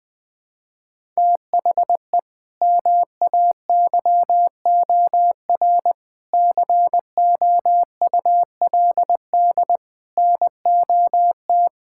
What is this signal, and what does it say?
Morse code, heard as a keyed tone, about 21 wpm, 700 Hz. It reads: THE MAYOR COULD NOT
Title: Sentences with 6-Word Phrases - Spoken Beginning - 20wpm